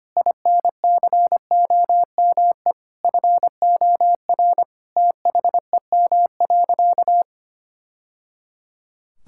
Morse code, heard as a keyed tone, about 25 wpm, 700 Hz. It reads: INCOME FOR THEM.
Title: Sentences from Top 1000 Words - Review Part 5 - 25wpm